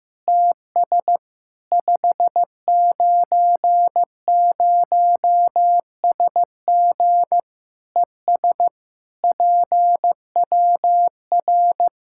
Morse code, heard as a keyed tone, about 16 wpm, 700 Hz. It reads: TS 590SG ES PWR
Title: Practice QSO - Exchange 2 - Info - 15wpm